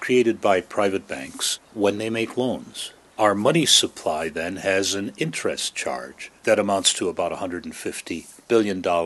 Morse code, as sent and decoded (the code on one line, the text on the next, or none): none